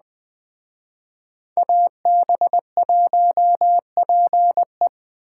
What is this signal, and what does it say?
Morse code, heard as a keyed tone, about 20 wpm, 700 Hz. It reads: AB1PE